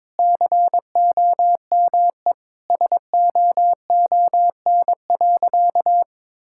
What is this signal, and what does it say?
Morse code, heard as a keyed tone, about 22 wpm, 700 Hz. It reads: COME SOON.
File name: practice / 22wpm / Sentences with 2-Word Phrases - 22wpm